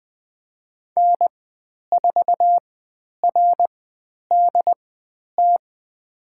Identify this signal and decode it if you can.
Morse code, heard as a keyed tone, about 20 wpm, spaced as for 10 wpm, 700 Hz. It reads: N4RDT
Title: 102 Lesson 17g - Letter u - Call Signs 20wpm